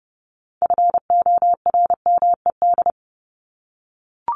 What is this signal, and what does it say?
Morse code, heard as a keyed tone, about 30 wpm, 700 Hz. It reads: FORMED
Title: ED Suffix - Top 100 Words with Base Words 30wpm